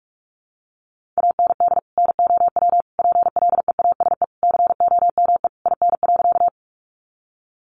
Morse code, heard as a keyed tone, about 44 wpm, 700 Hz. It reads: AND NOW PLEASE COME IN.
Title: Sentences from Top 1500 Words 45wpm